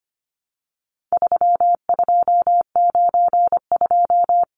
Morse code, heard as a keyed tone, about 25 wpm, 700 Hz. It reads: 3292
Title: Numbers - 4-Digit 25wpm